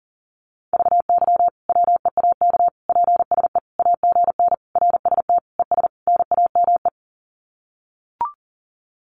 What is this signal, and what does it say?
Morse code, heard as a keyed tone, about 40 wpm, 700 Hz. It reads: VY WEAK PSE AGN RST ES NAME
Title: Common QSO Phrases 40wpm